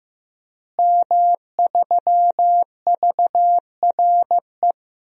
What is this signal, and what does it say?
Morse code, heard as a keyed tone, about 15 wpm, 700 Hz. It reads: M3VRE
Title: DXCC-100 Call Signs - 5 Characters - Repeated 3x - 15wpm